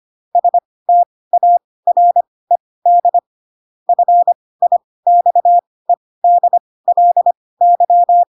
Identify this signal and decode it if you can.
Morse code, heard as a keyed tone, about 25 wpm, spaced as for 18 wpm, 700 Hz. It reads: STARED FIXEDLY
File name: Pinocchio ch3 12 25 ik1hge